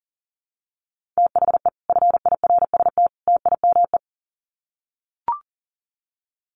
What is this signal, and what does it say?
Morse code, heard as a keyed tone, about 40 wpm, 700 Hz. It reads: THE FIRST TIME
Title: Sets of 3 Words 40wpm